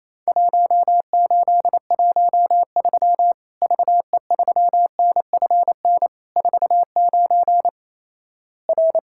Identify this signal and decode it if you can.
Morse code, heard as a keyed tone, about 28 wpm, 700 Hz. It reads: 1813 VE3NFN 49 R
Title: Worked All Europe DX Contest - QTC Exchange - 28wpm